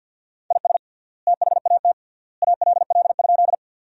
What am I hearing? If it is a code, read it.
Morse code, heard as a keyed tone, about 47 wpm, 700 Hz. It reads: IS THAT ALL?
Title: 3x Word-Spacing - Sentences from Top 100 Words - 50wpm